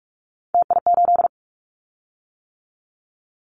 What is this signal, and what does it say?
Morse code, heard as a keyed tone, about 44 wpm, 700 Hz. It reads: TI8
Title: Callsign Prefixes for Top 100 DXCC - 45wpm